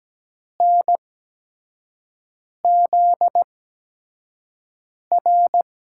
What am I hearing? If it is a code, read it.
Morse code, heard as a keyed tone, about 17 wpm, spaced as for 5 wpm, 700 Hz. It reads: NZR
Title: USA Call Signs - 5 Characters - 8x Character Spacing - 17wpm